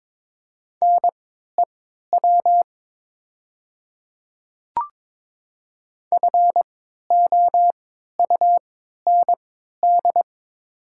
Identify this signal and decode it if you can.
Morse code, heard as a keyed tone, about 22 wpm, spaced as for 13 wpm, 700 Hz. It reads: NEW FOUND
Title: 3x Character Spacing - Top 500 Words - 22wpm